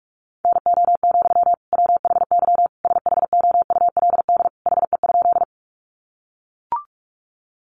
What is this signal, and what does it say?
Morse code, heard as a keyed tone, about 44 wpm, 700 Hz. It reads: NO, WHY SHOULD HE?
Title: Sentences from Top 300 Words 45wpm